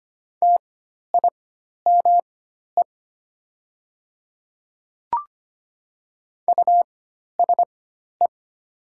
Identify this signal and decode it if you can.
Morse code, heard as a keyed tone, about 25 wpm, spaced as for 12 wpm, 700 Hz. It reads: TIME USE
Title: 4x Character Spacing - Top 200 Words - 25wpm